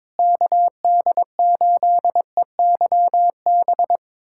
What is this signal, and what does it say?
Morse code, heard as a keyed tone, about 22 wpm, 700 Hz. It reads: KD8EYB
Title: USA Callsign DE Callsign - 6 Characters - 22wpm